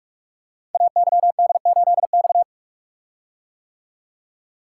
Morse code, heard as a keyed tone, about 44 wpm, 700 Hz. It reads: AYD8X
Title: Code Groups - 5 Letter:Numbers - 45wpm